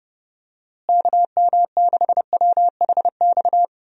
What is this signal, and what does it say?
Morse code, heard as a keyed tone, about 30 wpm, 700 Hz. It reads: KM6WHX
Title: USA Call Signs - 6 Characters - Repeated 3x - 30wpm